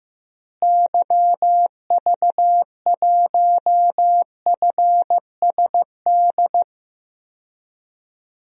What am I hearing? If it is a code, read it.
Morse code, heard as a keyed tone, about 15 wpm, 700 Hz. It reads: YV1FSD
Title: DXCC-100 Call Signs - 6 Characters - Repeated 2x - 15wpm